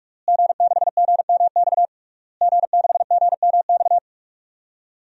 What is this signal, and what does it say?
Morse code, heard as a keyed tone, about 47 wpm, 700 Hz. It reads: G6GMX G6GMX
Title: DXCC-100 Call Signs - 5 Characters - Repeated 3x - 45wpm